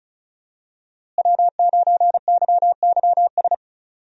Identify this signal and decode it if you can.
Morse code, heard as a keyed tone, about 36 wpm, 700 Hz. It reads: W9YYS
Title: USA Call Signs - 5 Characters - Repeated 2x - 35wpm